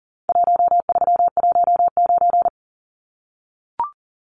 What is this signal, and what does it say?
Morse code, heard as a keyed tone, about 40 wpm, 700 Hz. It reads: 1319
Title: Numbers - Contest Style - 4-Digit 40wpm